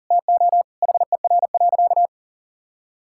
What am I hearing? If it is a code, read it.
Morse code, heard as a keyed tone, about 40 wpm, 700 Hz. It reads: TO HER.
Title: Sentences from Top 1000 Words - Review Part 5 - 40wpm